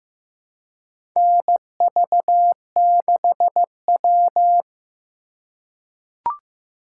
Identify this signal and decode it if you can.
Morse code, heard as a keyed tone, about 15 wpm, 700 Hz. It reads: NV6W
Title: USA Call Signs - 4 Characters - 4x Character Spacing - 15wpm